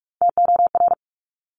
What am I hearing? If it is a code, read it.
Morse code, heard as a keyed tone, about 46 wpm, 700 Hz. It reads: TOR
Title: Top Three Letter Combinations in Words 45wpm